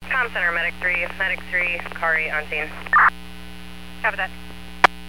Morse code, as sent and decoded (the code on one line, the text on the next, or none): none